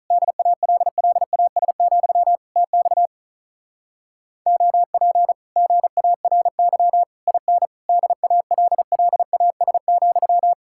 Morse code, heard as a keyed tone, about 48 wpm, 700 Hz. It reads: DALLAS, TX OP GARY IN DALLAS,